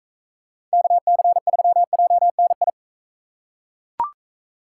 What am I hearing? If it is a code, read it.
Morse code, heard as a keyed tone, about 42 wpm, 700 Hz. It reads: KC3JNI